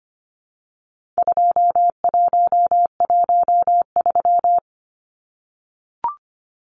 Morse code, heard as a keyed tone, about 25 wpm, 700 Hz. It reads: 2113
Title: Numbers - 4-Digit 25wpm